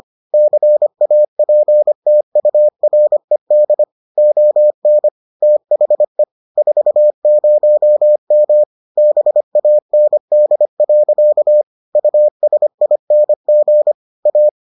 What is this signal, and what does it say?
Morse code, heard as a keyed tone, about 25 wpm, 600 Hz. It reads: CAPTURED ON THE 40M BAND. USING A